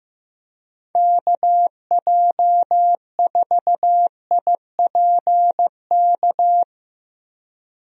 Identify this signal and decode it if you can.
Morse code, heard as a keyed tone, about 15 wpm, 700 Hz. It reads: KJ4IPK
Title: USA Call Signs - 6 Characters - 15wpm